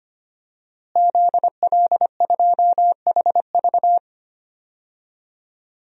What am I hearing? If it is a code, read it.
Morse code, heard as a keyed tone, about 25 wpm, 700 Hz. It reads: ZL2HV
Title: DXCC-100 Call Signs - 5 Characters - Repeated 3x - 25wpm